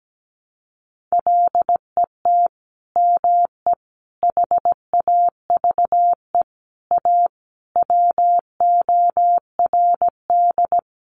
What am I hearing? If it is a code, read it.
Morse code, heard as a keyed tone, about 17 wpm, 700 Hz. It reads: LET ME HAVE A WORD
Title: Sentences from Top 900 Words 17wpm